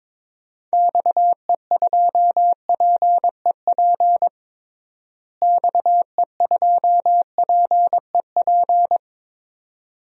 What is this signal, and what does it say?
Morse code, heard as a keyed tone, about 21 wpm, 700 Hz. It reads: XE2PEP XE2PEP
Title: DXCC-100 Call Signs - 6 Characters - Repeated 2x - 22wpm